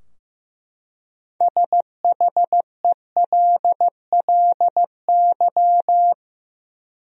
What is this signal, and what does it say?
Morse code, heard as a keyed tone, about 16 wpm, 700 Hz. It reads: SHELLY
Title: USA Common Names - 15wpm